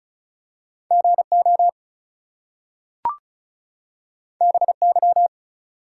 Morse code, heard as a keyed tone, about 36 wpm, 700 Hz. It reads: GO BY
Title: Two Letter Words - 35wpm